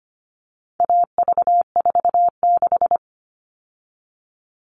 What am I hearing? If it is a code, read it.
Morse code, heard as a keyed tone, about 25 wpm, 700 Hz. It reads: AV46
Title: All Cut Numbers - 4-Digit 25wpm